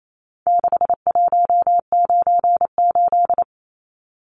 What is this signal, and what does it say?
Morse code, heard as a keyed tone, about 28 wpm, 700 Hz. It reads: 6198
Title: Numbers - 4-Digit 28wpm